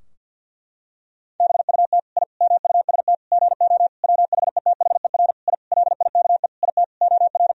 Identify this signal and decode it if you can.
Morse code, heard as a keyed tone, about 50 wpm, 700 Hz. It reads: BUT I MUST GO WHETHER I LIKE IT OR
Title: Sentences from Top 1000 Words - Review Part 3 - 50wpm